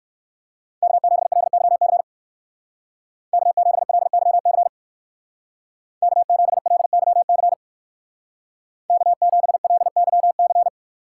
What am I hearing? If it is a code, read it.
Morse code, heard as a keyed tone, about 58 wpm, 700 Hz. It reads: K7LYC K7LYC K7LYC K7LYC